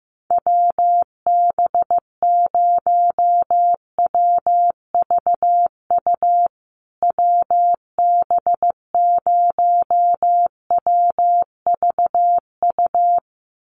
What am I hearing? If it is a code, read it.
Morse code, heard as a keyed tone, about 15 wpm, 700 Hz. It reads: WB0WVU WB0WVU
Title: USA Call Signs Repeated 2x 15wpm